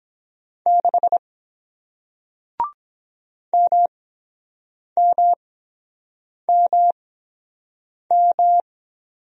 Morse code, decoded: 6 MMMM

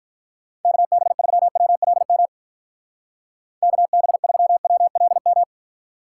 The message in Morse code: KB3WLM KB3WLM

